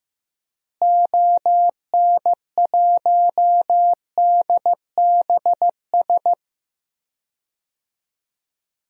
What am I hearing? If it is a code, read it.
Morse code, heard as a keyed tone, about 15 wpm, 700 Hz. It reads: ON1DBS
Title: DXCC-100 Call Signs - 6 Characters - Repeated 2x - 15wpm